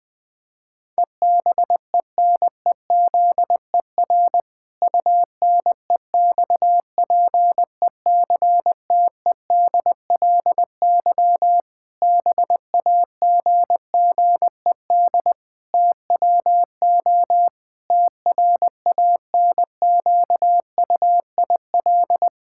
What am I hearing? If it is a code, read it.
Morse code, heard as a keyed tone, about 21 wpm, 700 Hz. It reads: EBENEZER UNEXPECTEDLY BAGGED TWO TRANQUIL
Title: Pangram Challenge v2 - 20wpm